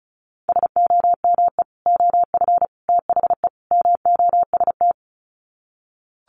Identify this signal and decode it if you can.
Morse code, heard as a keyed tone, about 34 wpm, 700 Hz. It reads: SOME OF THE MOST
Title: Sets of 4 Words 35wpm